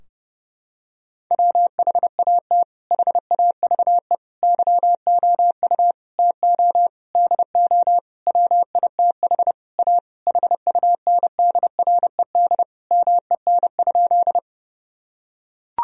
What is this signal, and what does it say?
Morse code, heard as a keyed tone, about 30 wpm, 700 Hz. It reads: WHAT HAVE YOU TO DO WITH A HUNDRED MEN?
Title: Sentences from Top 1000 Words - Review Part 2 - 30wpm